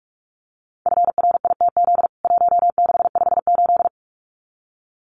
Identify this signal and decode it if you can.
Morse code, heard as a keyed tone, about 44 wpm, 700 Hz. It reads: FRITZ 1658
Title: CWT Exchange 45wpm